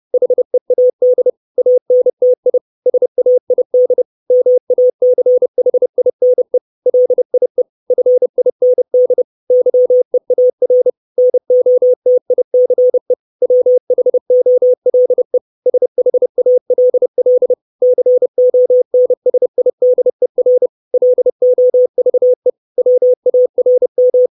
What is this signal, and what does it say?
Morse code, heard as a keyed tone, about 30 wpm, 500 Hz. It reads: HEAD ANTI SAID MACHINE LIE FIND YEAR NOTICE WHOLE SHALL CONSIDER LOVE WARM
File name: 632 Words 30WPM